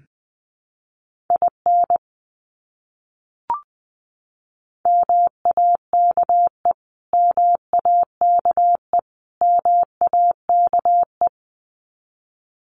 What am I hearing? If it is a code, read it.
Morse code, heard as a keyed tone, about 20 wpm, 700 Hz. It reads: IN MAKE MAKE MAKE
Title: Top 300 Words Repeated 20wpm